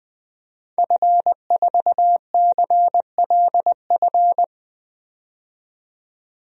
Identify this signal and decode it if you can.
Morse code, heard as a keyed tone, about 20 wpm, 700 Hz. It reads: F4CLF
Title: DXCC-100 Call Signs - 5 Characters - 20wpm